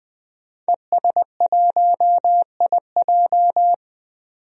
Morse code, heard as a keyed tone, about 20 wpm, 700 Hz. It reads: ES1IJ
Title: DXCC-100 Call Signs - 5 Characters - 20wpm